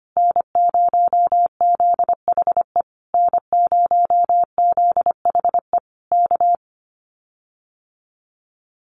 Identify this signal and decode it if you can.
Morse code, heard as a keyed tone, about 25 wpm, 700 Hz. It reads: N0ZHE N0ZHE K
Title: Common QSO Phrases Encore 25wpm